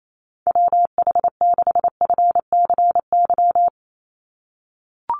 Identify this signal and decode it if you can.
Morse code, heard as a keyed tone, about 28 wpm, 700 Hz. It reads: WH6FCY
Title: USA Call Signs 28wpm